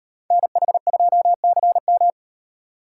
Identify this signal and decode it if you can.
Morse code, heard as a keyed tone, about 35 wpm, 700 Hz. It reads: NH2CM